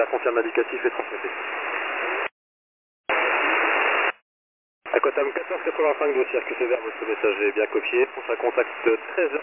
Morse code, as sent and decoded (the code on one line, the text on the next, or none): none